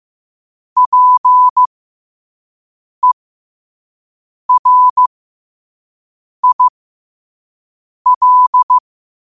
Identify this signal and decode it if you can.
Morse code, heard as a keyed tone, about 15 wpm, spaced as for 5 wpm, 1000 Hz. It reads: PERIL